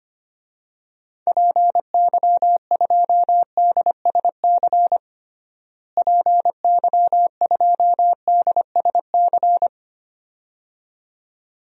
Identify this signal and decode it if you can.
Morse code, heard as a keyed tone, about 25 wpm, 700 Hz. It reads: PY2DSC PY2DSC
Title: DXCC-100 Call Signs - 6 Characters - Repeated 2x - 25wpm